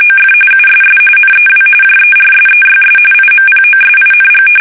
RTTY, about 45.45 baud. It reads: OF USS NEW JERSEYS IMPOSING
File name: fsk850 / battleshipnj